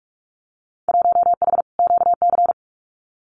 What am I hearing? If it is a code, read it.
Morse code, heard as a keyed tone, about 44 wpm, 700 Hz. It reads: 1H QC
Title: Winter Field Day Exchanges 45wpm